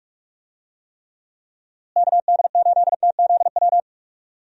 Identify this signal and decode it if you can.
Morse code, heard as a keyed tone, about 42 wpm, 700 Hz. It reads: KD8TZW
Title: USA Call Signs - 6 Characters - Repeated 3x - 45wpm